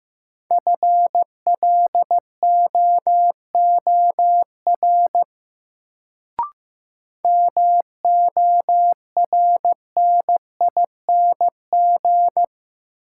Morse code, heard as a keyed tone, about 15 wpm, 700 Hz. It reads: FLOOR MORNING
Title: Sets of 3 Words - New Word Drill - 15wpm